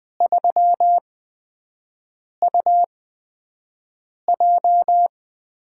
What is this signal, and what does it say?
Morse code, heard as a keyed tone, about 20 wpm, spaced as for 5 wpm, 700 Hz. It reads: 3UJ
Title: USA Call Signs - 4 Characters - 8x Character Spacing - 20wpm